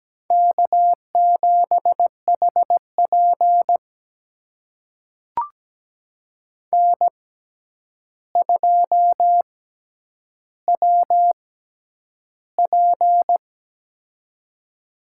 Morse code, heard as a keyed tone, about 17 wpm, spaced as for 9 wpm, 700 Hz. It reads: K7HP N2WP
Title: USA Call Signs - 4 Characters - 6x Character Spacing - 17wpm